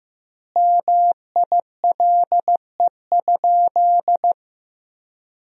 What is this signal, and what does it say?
Morse code, heard as a keyed tone, about 15 wpm, 700 Hz. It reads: MILE?
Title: 2x Word-Spacing - Sentences from Top 400 Words - 15wpm